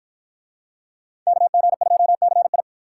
Morse code, heard as a keyed tone, about 50 wpm, 700 Hz. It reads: KG2CI